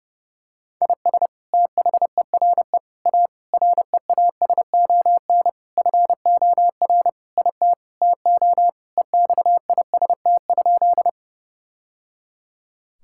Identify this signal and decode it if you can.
Morse code, heard as a keyed tone, about 30 wpm, 700 Hz. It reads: IS THERE A REASON FOR IT TO EXIST?
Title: Sentences from Top 1000 Words - Review Part 5 - 30wpm